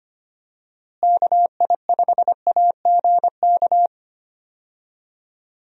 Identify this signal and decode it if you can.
Morse code, heard as a keyed tone, about 25 wpm, 700 Hz. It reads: KI5AGK
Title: USA Call Signs - 6 Characters - Repeated 2x - 25wpm